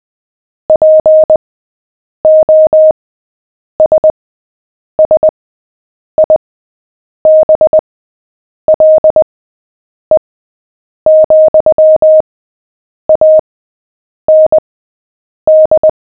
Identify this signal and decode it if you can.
Morse code, heard as a keyed tone, about 20 wpm, spaced as for 8 wpm, 620 Hz. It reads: POSSIBLE, AND
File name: cw / fists / Short-Story-7-8wpm